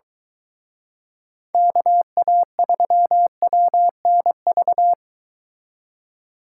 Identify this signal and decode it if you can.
Morse code, heard as a keyed tone, about 23 wpm, 700 Hz. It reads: KA3WNV